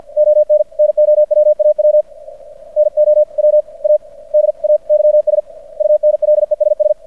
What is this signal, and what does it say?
Morse code, heard as a keyed tone, about 50 wpm, spaced as for 41 wpm, 600 Hz. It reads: ON NOWAW NOWADAYS UNLESS
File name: archived / CRSnet